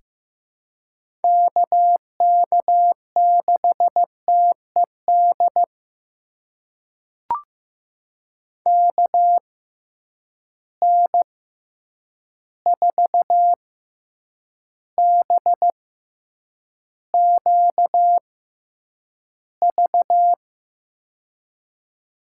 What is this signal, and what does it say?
Morse code, heard as a keyed tone, about 16 wpm, spaced as for 8 wpm, 700 Hz. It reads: KK6TED KN4BQV
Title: USA Call Signs - 6 Characters - 6x Character Spacing - 15wpm